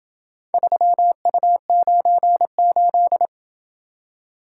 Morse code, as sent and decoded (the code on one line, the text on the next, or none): ...-- ..- ----. ---..
3U98